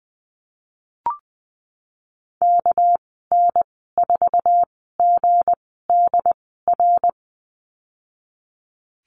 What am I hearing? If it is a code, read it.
Morse code, heard as a keyed tone, about 19 wpm, spaced as for 14 wpm, 700 Hz. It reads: KN4GDR
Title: 238 Lesson 38g - Letter q - Call Signs 20wpm